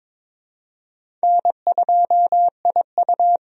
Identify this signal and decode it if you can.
Morse code, heard as a keyed tone, about 22 wpm, 700 Hz. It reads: N2IU
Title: USA Call Signs - 4 Characters - Repeated 2x - 22wpm